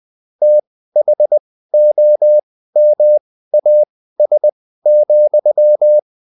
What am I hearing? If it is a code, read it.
Morse code, heard as a keyed tone, about 20 wpm, spaced as for 14 wpm, 600 Hz. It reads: THOMAS,